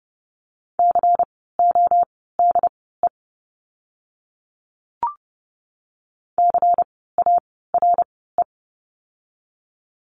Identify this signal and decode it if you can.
Morse code, heard as a keyed tone, about 30 wpm, spaced as for 17 wpm, 700 Hz. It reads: CODE CARE
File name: Lesson 16f - Letter c - Words 30wpm